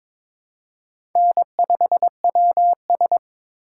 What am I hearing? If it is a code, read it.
Morse code, heard as a keyed tone, about 22 wpm, 700 Hz. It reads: N5WS
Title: USA Call Signs - 4 Characters - 3x Character Spacing - 22wpm